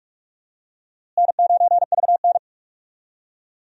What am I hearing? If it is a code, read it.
Morse code, heard as a keyed tone, about 44 wpm, 700 Hz. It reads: N9VN
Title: USA Call Signs - 4 Characters - Repeated 3x - 45wpm